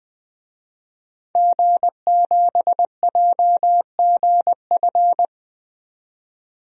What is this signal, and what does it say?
Morse code, heard as a keyed tone, about 20 wpm, 700 Hz. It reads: G7JGF
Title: DXCC-100 Call Signs - 5 Characters - 20wpm